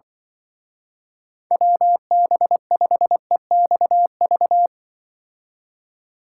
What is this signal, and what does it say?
Morse code, heard as a keyed tone, about 24 wpm, 700 Hz. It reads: WB5EXV
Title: Speed Racing - USA 2x3 Call Signs 20wpm